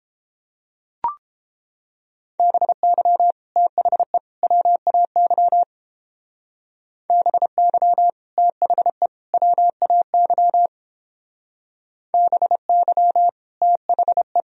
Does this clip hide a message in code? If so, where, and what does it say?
Yes: Morse code from 2.4 to 14.4 s, heard as a keyed tone, about 31 wpm, 700 Hz. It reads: BY THE WAY BY THE WAY BY THE